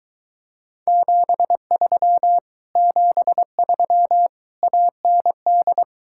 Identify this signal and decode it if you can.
Morse code, heard as a keyed tone, about 22 wpm, 700 Hz. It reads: 73 73 AND